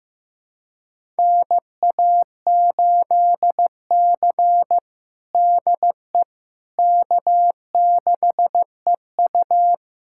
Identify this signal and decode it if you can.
Morse code, heard as a keyed tone, about 15 wpm, 700 Hz. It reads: NA8C DE K6EU
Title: USA Callsign DE Callsign - 4 Characters - 15wpm